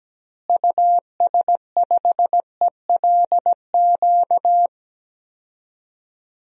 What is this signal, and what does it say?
Morse code, heard as a keyed tone, about 17 wpm, 700 Hz. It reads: US5ELQ
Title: DXCC-100 Call Signs - 6 Characters - Repeated 2x - 17wpm